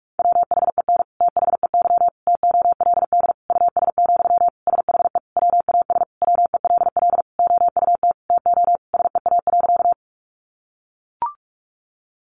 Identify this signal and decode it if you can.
Morse code, heard as a keyed tone, about 44 wpm, 700 Hz. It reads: WHEN THEY TOLD US, SHE WAS WELL OUT TO SEA.
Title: Sentences from Top 400 Words 45wpm